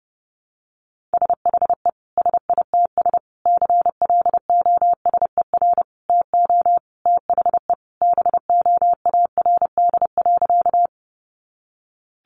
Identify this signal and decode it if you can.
Morse code, heard as a keyed tone, about 30 wpm, 700 Hz. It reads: SHE SITS CLOSER TO THE BOARD.
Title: Additional Easy Phrases 30wpm